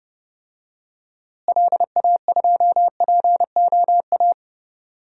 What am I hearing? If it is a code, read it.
Morse code, heard as a keyed tone, about 30 wpm, 700 Hz. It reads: LA2POA
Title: DXCC-100 Call Signs - 6 Characters - Repeated 2x - 30wpm